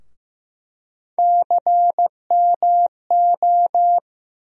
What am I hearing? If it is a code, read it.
Morse code, heard as a keyed tone, about 15 wpm, 700 Hz. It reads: CMO
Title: USA Call Sign Postfixes - 3 Letters 15wpm